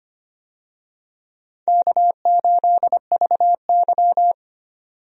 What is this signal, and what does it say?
Morse code, heard as a keyed tone, about 25 wpm, 700 Hz. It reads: K8VY
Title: USA Call Signs - 4 Characters - 25wpm